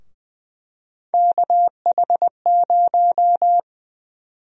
Morse code, heard as a keyed tone, about 20 wpm, 700 Hz. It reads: KH0
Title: USA Call Sign Prefixes - 20wpm